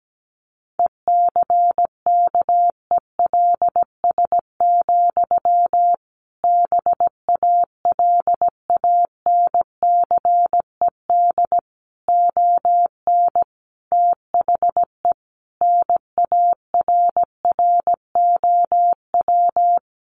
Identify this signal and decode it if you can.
Morse code, heard as a keyed tone, about 17 wpm, 700 Hz. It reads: ECKELS, BALANCED ON THE NARROW
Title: A Sound Of Thunder 17wpm